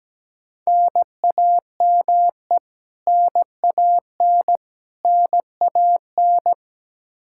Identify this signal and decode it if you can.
Morse code, heard as a keyed tone, about 17 wpm, 700 Hz. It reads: NAME NAN NAN
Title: K1USN SST QSO Practice 17wpm